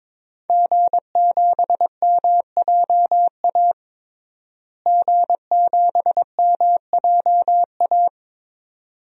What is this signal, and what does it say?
Morse code, heard as a keyed tone, about 22 wpm, 700 Hz. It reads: G7MJA G7MJA
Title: DXCC-100 Call Signs - 5 Characters - Repeated 2x - 22wpm